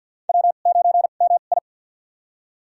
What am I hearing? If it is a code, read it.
Morse code, heard as a keyed tone, about 46 wpm, spaced as for 35 wpm, 700 Hz. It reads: W9MI
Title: USA Call Signs - 4 Characters - 2x Character Spacing - 50wpm